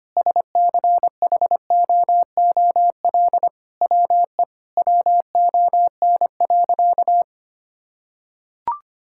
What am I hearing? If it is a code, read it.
Morse code, heard as a keyed tone, about 25 wpm, 700 Hz. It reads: SCHOOL WE WON.
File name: practice / 25wpm / Sentences with 9-Word Phrases - 25wpm